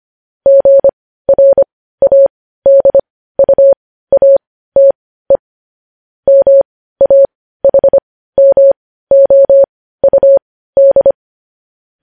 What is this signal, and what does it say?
Morse code, heard as a keyed tone, about 25 wpm, spaced as for 15 wpm, 550 Hz. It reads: GRADUATE MAHMOUD
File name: podcast / News headlines-1x-25x15